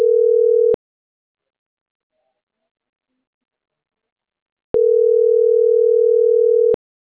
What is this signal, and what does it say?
Ringback tone (North America), 440 + 480 Hz.